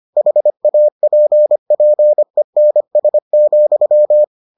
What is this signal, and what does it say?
Morse code, heard as a keyed tone, about 25 wpm, 600 Hz. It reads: HAPPENS,